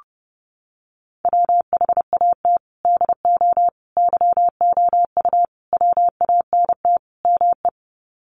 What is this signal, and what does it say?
Morse code, heard as a keyed tone, about 30 wpm, 700 Hz. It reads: WHAT DO YOU WANT ME
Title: Sets of 5 Words Encore2 30wpm